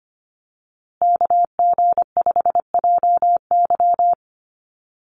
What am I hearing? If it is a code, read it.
Morse code, heard as a keyed tone, about 25 wpm, 700 Hz. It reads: KG5JY
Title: USA Call Signs Repeated 3x 25wpm